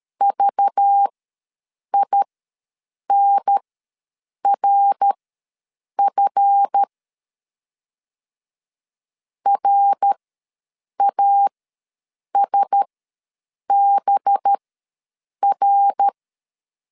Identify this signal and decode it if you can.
Morse code, heard as a keyed tone, about 13 wpm, spaced as for 7 wpm, 800 Hz. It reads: VINRF RASBR
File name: LEZIONI / 7 GF9Z / 7 GF ws 12wpm 3140